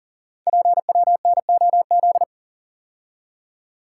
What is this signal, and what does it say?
Morse code, heard as a keyed tone, about 38 wpm, 700 Hz. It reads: PWNOZ
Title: Code Groups - 5 Letters - 40wpm